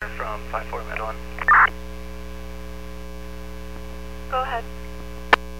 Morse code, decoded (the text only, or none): none